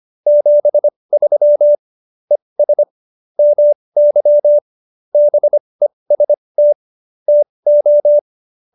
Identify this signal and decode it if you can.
Morse code, heard as a keyed tone, about 25 wpm, spaced as for 20 wpm, 590 Hz. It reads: 73 ES MY BEST TO